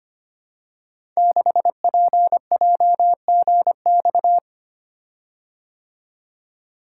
Morse code, heard as a keyed tone, about 25 wpm, 700 Hz. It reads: 6PJGX
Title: Code Groups - 5 Letter:Numbers - 25wpm